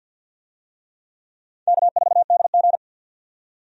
Morse code, heard as a keyed tone, about 52 wpm, 700 Hz. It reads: K4DG
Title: USA Call Signs - 4 Characters - 50wpm